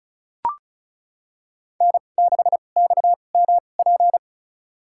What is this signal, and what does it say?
Morse code, heard as a keyed tone, about 36 wpm, spaced as for 25 wpm, 700 Hz. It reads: N6XMP
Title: USA Call Signs - 5 Characters - 2x Character Spacing - 35wpm